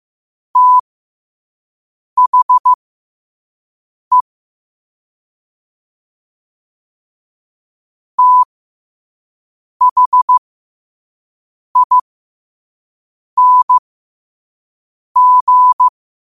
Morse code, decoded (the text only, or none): THE THING